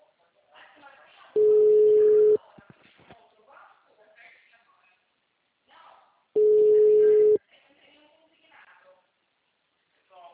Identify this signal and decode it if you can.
Ringback tone (Europe, 425 Hz), with speech.